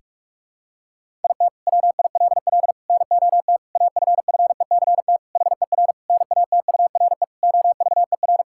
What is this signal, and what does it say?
Morse code, heard as a keyed tone, about 44 wpm, 700 Hz. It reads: IT WILL NOT AFFECT HER NATURE OVER
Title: Sentences from Top 1000 Words - Review Part 6 - 45wpm